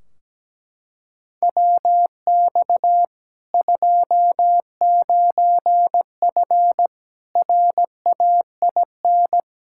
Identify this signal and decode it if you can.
Morse code, heard as a keyed tone, about 17 wpm, 700 Hz. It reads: WX 29F RAIN
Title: K1USN SST QSO Practice 17wpm